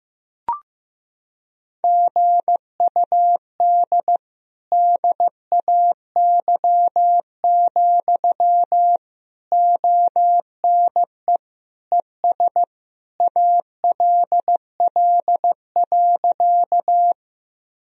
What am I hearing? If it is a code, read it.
Morse code, heard as a keyed tone, about 15 wpm, 700 Hz. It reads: GUD DAY, ONE ES ALL.
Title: Sentences from Top 200 Words and Abbreviations 15wpm